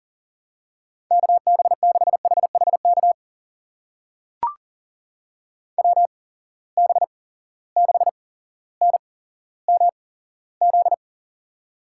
Morse code, heard as a keyed tone, about 40 wpm, spaced as for 17 wpm, 700 Hz. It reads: KB6HHK WB6NMZ
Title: USA Call Signs - 6 Characters - 8x Character Spacing - 40wpm